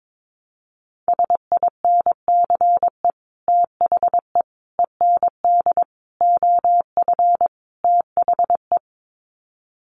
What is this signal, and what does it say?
Morse code, heard as a keyed tone, about 22 wpm, 700 Hz. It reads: SINCE THE END OF THE
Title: Sets of 5 Words Encore 22wpm